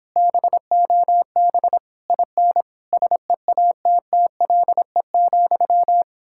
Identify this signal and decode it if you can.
Morse code, heard as a keyed tone, about 29 wpm, 700 Hz. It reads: BOB IN SEATTLE,